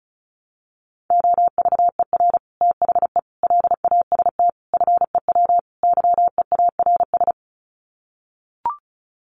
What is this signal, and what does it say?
Morse code, heard as a keyed tone, about 34 wpm, 700 Hz. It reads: OVER THE LAST FEW YEARS
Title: Sets of 5 Words Encore 35wpm